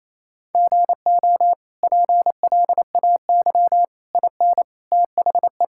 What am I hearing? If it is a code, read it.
Morse code, heard as a keyed tone, about 28 wpm, 700 Hz. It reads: GO PLAY IN THE
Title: Sentences from Top 1000 Words - Review Part 5 - 28wpm